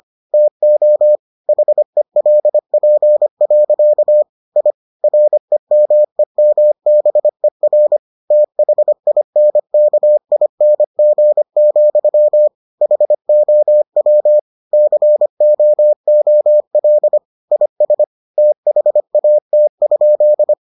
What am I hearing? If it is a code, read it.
Morse code, heard as a keyed tone, about 25 wpm, 600 Hz. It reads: TO HELP. I REMEMBER THINKING, HOW COOL IS THAT?